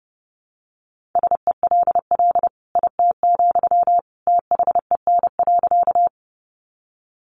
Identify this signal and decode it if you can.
Morse code, heard as a keyed tone, about 28 wpm, 700 Hz. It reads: SELL IT, THEN.
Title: Sentences from Top 1100 Words 30wpm